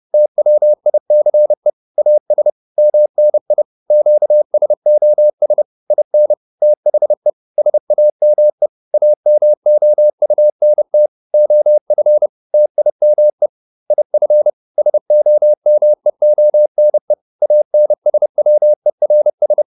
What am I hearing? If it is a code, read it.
Morse code, heard as a keyed tone, about 30 wpm, 600 Hz. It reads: TWICE AS MNI QSOS IN THE SAME AMOUNT OF TIME IF SOMEONE ANSWERS